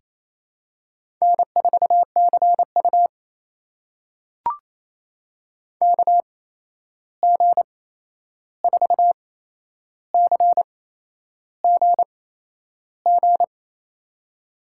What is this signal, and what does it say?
Morse code, heard as a keyed tone, about 28 wpm, spaced as for 11 wpm, 700 Hz. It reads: N4CU KG4CGG
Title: USA Call Signs - 8x Character Spacing - 28wpm